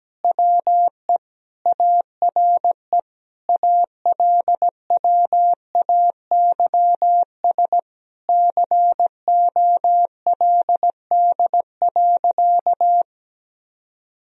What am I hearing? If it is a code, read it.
Morse code, heard as a keyed tone, about 17 wpm, 700 Hz. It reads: WE ARE ALWAYS COLD.
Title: Sentences from Top 1000 Words - Review Part 1 - 17wpm